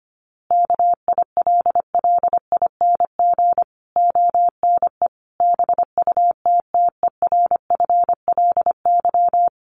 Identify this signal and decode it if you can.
Morse code, heard as a keyed tone, about 25 wpm, 700 Hz. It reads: KILLING ONE BUTTERFLY
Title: A Sound Of Thunder 25wpm